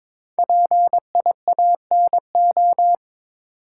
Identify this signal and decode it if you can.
Morse code, heard as a keyed tone, about 22 wpm, 700 Hz. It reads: PIANO
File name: 5 Letter Words - First 3 Letters Given - 22wpm